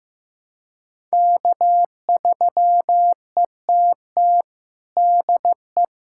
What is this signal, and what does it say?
Morse code, heard as a keyed tone, about 16 wpm, 700 Hz. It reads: K3ETT DE
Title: USA Callsign DE Callsign - 5 Characters - 15wpm